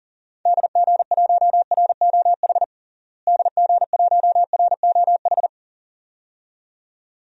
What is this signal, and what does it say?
Morse code, heard as a keyed tone, about 40 wpm, 700 Hz. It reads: DG1ROH DG1ROH
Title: DXCC-100 Call Signs - 6 Characters - Repeated 2x - 40wpm